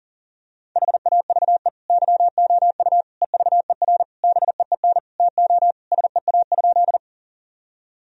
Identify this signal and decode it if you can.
Morse code, heard as a keyed tone, about 40 wpm, 700 Hz. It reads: HAVE YOU EVER BEEN TO SEA?
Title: Sentences from Top 1000 Words - Review Part 1 - 40wpm